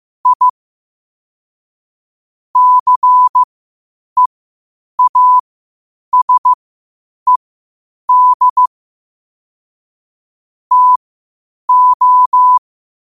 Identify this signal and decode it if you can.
Morse code, heard as a keyed tone, about 15 wpm, spaced as for 8 wpm, 1000 Hz. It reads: I CEASED TO
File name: qotdfiles / 8wpm